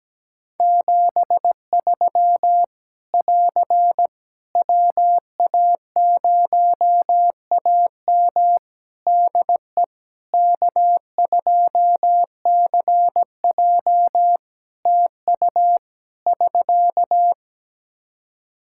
Morse code, heard as a keyed tone, about 17 wpm, 700 Hz. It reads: 73 <AR> WA0AM DE K2CJ TU <SK>
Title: K1USN SST QSO Practice 17wpm